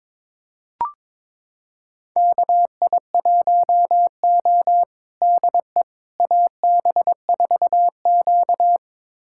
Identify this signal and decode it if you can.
Morse code, heard as a keyed tone, about 22 wpm, 700 Hz. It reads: KI1O DE AB4Q
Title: USA Callsign DE Callsign - 4 Characters - 22wpm